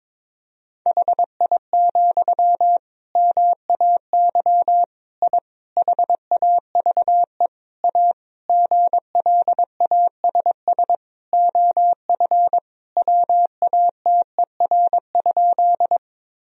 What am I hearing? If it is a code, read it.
Morse code, heard as a keyed tone, about 21 wpm, 700 Hz. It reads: HI, MAY I HAVE A GLASS OF WATER?
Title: Sentences from Top 1000 Words - Review Part 3 - 22wpm